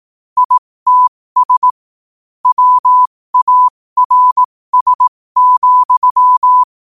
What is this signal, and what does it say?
Morse code, heard as a keyed tone, about 18 wpm, spaced as for 15 wpm, 1000 Hz. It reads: ITS WARS,
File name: 15wpm